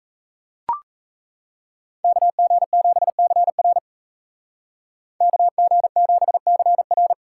Morse code, heard as a keyed tone, about 39 wpm, 700 Hz. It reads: KG7CR KG7CR